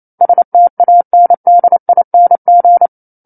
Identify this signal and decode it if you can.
Morse code, heard as a keyed tone, about 28 wpm, 700 Hz. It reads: STANDING
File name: 09-Chapter-09 28 WPM